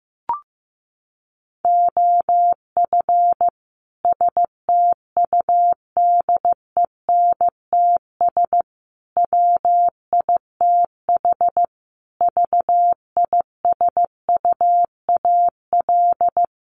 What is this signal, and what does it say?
Morse code, heard as a keyed tone, about 15 wpm, 700 Hz. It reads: OF STUDENTS WITH VISUAL
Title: Sets of 5 Words Encore 15wpm